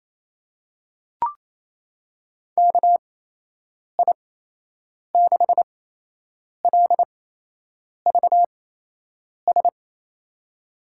Morse code, heard as a keyed tone, about 28 wpm, spaced as for 8 wpm, 700 Hz. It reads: KI6LVS
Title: USA Call Signs - 6 Characters - 8x Character Spacing - 28wpm